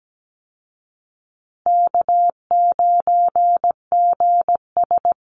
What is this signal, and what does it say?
Morse code, heard as a keyed tone, about 17 wpm, 700 Hz. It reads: K9GS
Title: Easy Call Signs Repeated 3x - 17wpm